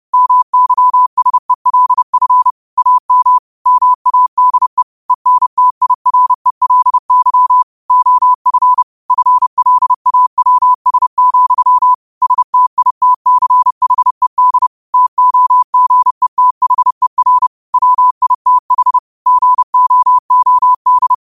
Morse code, heard as a keyed tone, about 30 wpm, 1000 Hz. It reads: MYSELF AM MADE ENTIRELY OF FLAWS, STITCHED TOGETHER WITH GOOD
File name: qotdfiles / 30wpm